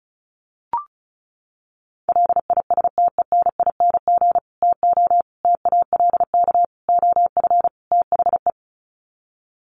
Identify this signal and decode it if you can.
Morse code, heard as a keyed tone, about 34 wpm, 700 Hz. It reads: LISTENING TO TALK OF THE
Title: Sets of 5 Words 35wpm